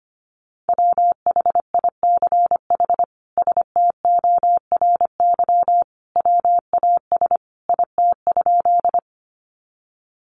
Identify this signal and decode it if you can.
Morse code, heard as a keyed tone, about 25 wpm, 700 Hz. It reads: WHICH STORY WAS IT?
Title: Sentences from Top 300 Words 25wpm